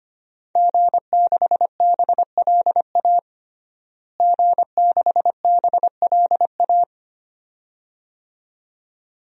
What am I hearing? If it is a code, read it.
Morse code, heard as a keyed tone, about 25 wpm, 700 Hz. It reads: G6BLA G6BLA
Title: DXCC-100 Call Signs - 5 Characters - Repeated 2x - 25wpm